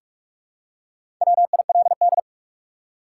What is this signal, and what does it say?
Morse code, heard as a keyed tone, about 44 wpm, 700 Hz. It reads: WILD
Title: Four Letter Words - 45wpm